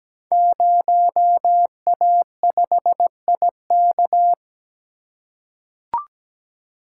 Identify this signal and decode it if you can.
Morse code, heard as a keyed tone, about 17 wpm, 700 Hz. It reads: 0A5IK
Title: Code Groups - 5 Letter:Numbers - 17wpm